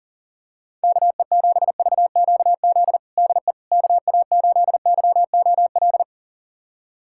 Morse code, heard as a keyed tone, about 40 wpm, 700 Hz. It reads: KE7VQZ DE KA8YOL
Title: USA Callsign DE Callsign - 6 Characters - 40wpm